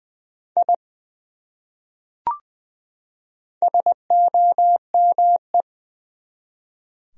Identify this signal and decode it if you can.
Morse code, heard as a keyed tone, about 20 wpm, 700 Hz. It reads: I SOME
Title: Top 100 Words 20wpm